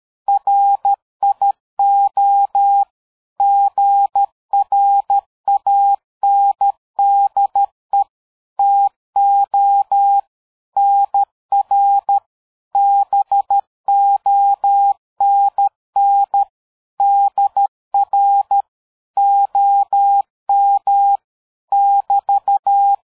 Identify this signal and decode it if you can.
Morse code, heard as a keyed tone, about 13 wpm, 800 Hz. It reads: RIO GRANDE TO NR BONN DR OM <BT>
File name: dk5ke / QSO 60BpM